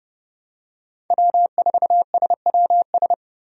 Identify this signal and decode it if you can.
Morse code, heard as a keyed tone, about 29 wpm, 700 Hz. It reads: W4SWS
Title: USA Call Signs - 5 Characters - 30wpm